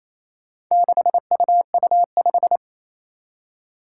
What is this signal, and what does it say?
Morse code, heard as a keyed tone, about 28 wpm, 700 Hz. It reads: 6UU5
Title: Four Challenging Letter-Number Combinations 28wpm